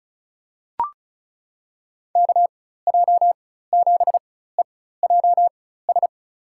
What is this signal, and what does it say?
Morse code, heard as a keyed tone, about 34 wpm, spaced as for 16 wpm, 690 Hz. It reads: KJ7EJS
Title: USA Call Signs - 6 Characters - 4x Character Spacing - 35wpm